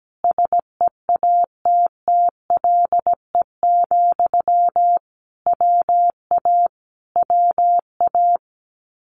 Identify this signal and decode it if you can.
Morse code, heard as a keyed tone, about 17 wpm, 700 Hz. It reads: SEATTLE, WA WA
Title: Common QSO Phrases Encore 17wpm